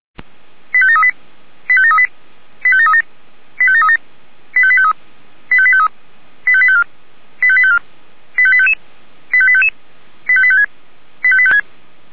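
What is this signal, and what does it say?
Five-tone selective call (ZVEI), heard as quick runs of five tones, tones of about 70 ms. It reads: 85738 85738 85737 85737 85853 85853 85854 85854 85855 85855 85856 85856
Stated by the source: Five-tone ZVEI selective call received on an SDR.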